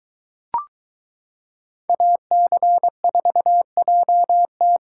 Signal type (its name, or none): Morse code